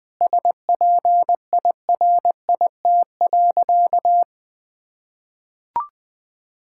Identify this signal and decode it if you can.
Morse code, heard as a keyed tone, about 20 wpm, 700 Hz. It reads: SPIRIT.
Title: Sentences with 9-Word Phrases - 20wpm